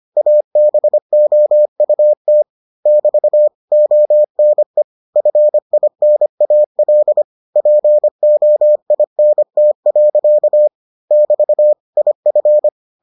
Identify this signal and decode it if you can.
Morse code, heard as a keyed tone, about 25 wpm, 600 Hz. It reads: ABOUT <BT> ONE FINAL POINT. <BT> IF